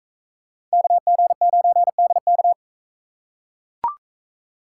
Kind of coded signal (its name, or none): Morse code